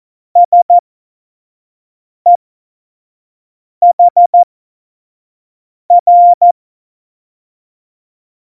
Morse code, heard as a keyed tone, about 14 wpm, spaced as for 5 wpm, 700 Hz. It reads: SEHR
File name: Morsekurs / Witze / Witz 12 Mai 2020, 25 F70